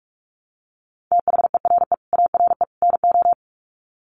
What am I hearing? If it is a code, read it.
Morse code, heard as a keyed tone, about 44 wpm, 700 Hz. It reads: THERE ARE NO
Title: Sets of 3 Words 45wpm